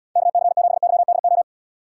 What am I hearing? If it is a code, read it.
Morse code, heard as a keyed tone, about 77 wpm, 700 Hz. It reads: KC7YDO